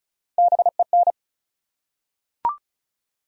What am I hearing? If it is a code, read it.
Morse code, heard as a keyed tone, about 34 wpm, 700 Hz. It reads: BEN
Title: USA Common Names - 35wpm